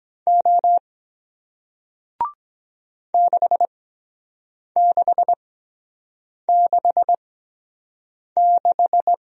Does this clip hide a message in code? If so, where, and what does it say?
Yes: Morse code from 0.3 to 9.1 s, heard as a keyed tone, about 23 wpm, spaced as for 7 wpm, 700 Hz. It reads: O 6666